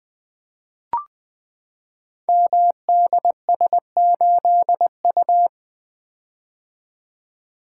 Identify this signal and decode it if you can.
Morse code, heard as a keyed tone, about 20 wpm, 700 Hz. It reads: MDS8U
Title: Code Groups - 5 Letter:Numbers - 20wpm